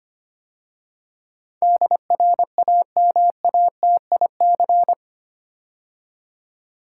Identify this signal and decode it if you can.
Morse code, heard as a keyed tone, about 25 wpm, 700 Hz. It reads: DRAMATIC
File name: 8 Letter Words - First 3 Letters Given - 25wpm